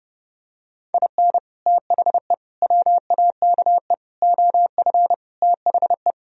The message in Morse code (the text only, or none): IN THE WAKE OF THE